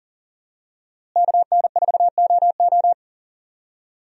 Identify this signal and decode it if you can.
Morse code, heard as a keyed tone, about 40 wpm, 700 Hz. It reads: KN4OO
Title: USA Call Signs - 5 Characters - Repeated 3x - 40wpm